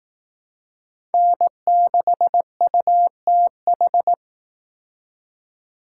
Morse code, heard as a keyed tone, about 18 wpm, 700 Hz. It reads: N6UTH
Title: Speed Racing - USA 1x3 Call Signs 15wpm